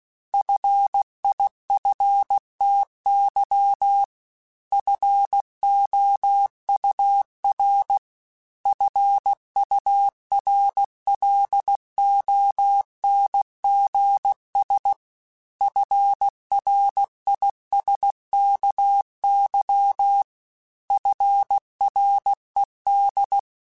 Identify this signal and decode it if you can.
Morse code, heard as a keyed tone, about 16 wpm, 800 Hz. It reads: FIFTY FOUR FURLONGS FRISKY FRED
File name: F Runs / F Runs mp3s / F Runs @15